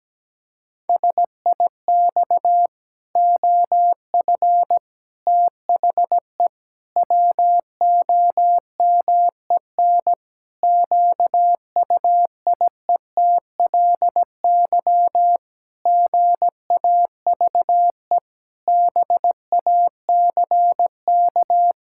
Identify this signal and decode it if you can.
Morse code, heard as a keyed tone, about 17 wpm, 700 Hz. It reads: SIX OF THE WOMEN QUIETLY GAVE BACK
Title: Pangram Challenge v2 - 17wpm